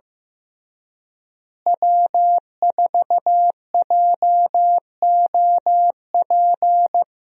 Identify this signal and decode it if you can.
Morse code, heard as a keyed tone, about 15 wpm, 700 Hz. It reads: W4JOP
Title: USA Call Signs - 5 Characters - Repeated 3x - 15wpm